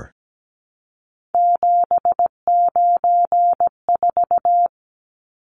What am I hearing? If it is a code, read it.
Morse code, heard as a keyed tone, about 17 wpm, 700 Hz. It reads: 794
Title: Numbers - 3-Digit 17wpm